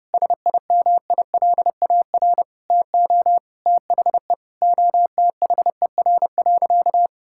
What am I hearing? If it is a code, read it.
Morse code, heard as a keyed tone, about 31 wpm, 700 Hz. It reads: SIMILAR TO THE OTHER.
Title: Sentences from Top 1000 Words - Review Part 5 - 30wpm